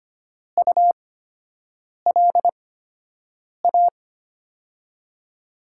Morse code, heard as a keyed tone, about 25 wpm, spaced as for 7 wpm, 700 Hz. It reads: ULA
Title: USA Call Signs - 8x Character Spacing - 25wpm